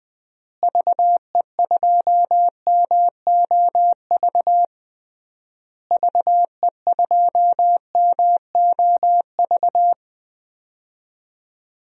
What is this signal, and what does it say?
Morse code, heard as a keyed tone, about 21 wpm, 700 Hz. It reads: VE2MOV VE2MOV
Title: DXCC-100 Call Signs - 6 Characters - Repeated 3x - 20wpm